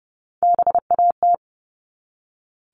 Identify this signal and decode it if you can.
Morse code, heard as a keyed tone, about 30 wpm, 700 Hz. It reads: BAT
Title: Top 1251-1500 Words 30wpm